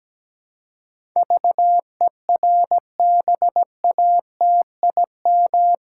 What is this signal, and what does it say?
Morse code, heard as a keyed tone, about 17 wpm, 700 Hz. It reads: VERBATIM
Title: Pangram Challenge v2 - 17wpm